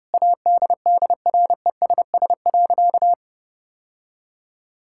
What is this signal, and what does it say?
Morse code, heard as a keyed tone, about 30 wpm, 700 Hz. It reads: ADDRESS.
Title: Sentences from Top 1000 Words - Review Part 6 - 30wpm